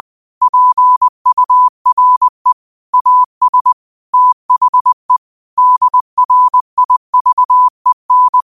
Morse code, heard as a keyed tone, about 20 wpm, 1000 Hz. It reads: PURE AS THE DRIVEN